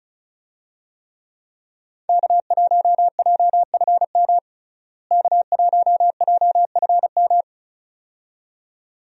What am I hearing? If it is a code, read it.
Morse code, heard as a keyed tone, about 34 wpm, 700 Hz. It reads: K1JFM K1JFM
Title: USA Call Signs - 5 Characters - Repeated 2x - 35wpm